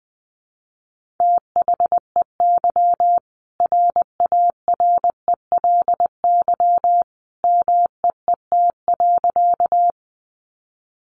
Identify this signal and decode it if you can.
Morse code, heard as a keyed tone, about 20 wpm, 700 Hz. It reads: THEY RARELY MEET.
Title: Sentences from Top 1400 Words 20wpm